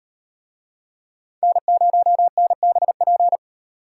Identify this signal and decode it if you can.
Morse code, heard as a keyed tone, about 38 wpm, 700 Hz. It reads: N0NBP